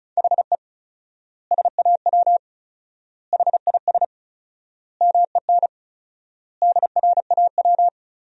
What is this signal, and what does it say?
Morse code, heard as a keyed tone, about 36 wpm, 700 Hz. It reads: HE SAW HIS MEN DRAW
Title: 4x Word-Spacing - Sentences from Top 300 Words - 35wpm